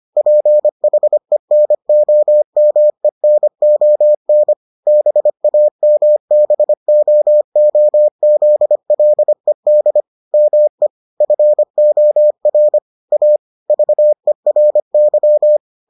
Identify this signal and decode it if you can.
Morse code, heard as a keyed tone, about 25 wpm, 600 Hz. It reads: PHENOMENON BAMBOOZLED ME FOR A VERY